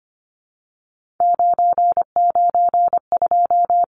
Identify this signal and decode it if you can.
Morse code, heard as a keyed tone, about 25 wpm, 690 Hz. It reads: 992
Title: Numbers - 3-Digit 25wpm